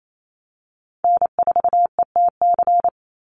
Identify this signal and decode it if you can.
Morse code, heard as a keyed tone, about 28 wpm, 700 Hz. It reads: N4ETC
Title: Call Signs - 28wpm